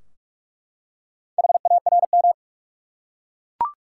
Morse code, heard as a keyed tone, about 46 wpm, 700 Hz. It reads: HARM
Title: Four Letter Words - 45wpm